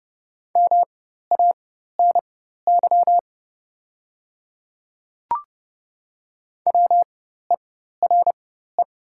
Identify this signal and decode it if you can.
Morse code, heard as a keyed tone, about 30 wpm, spaced as for 14 wpm, 700 Hz. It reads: MANY WERE
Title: 4x Character Spacing - Top 100 Words - 30wpm